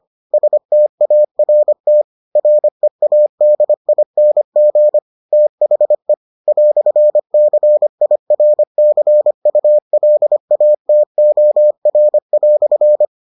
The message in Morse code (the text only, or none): START READING THE "CIRCULATOR"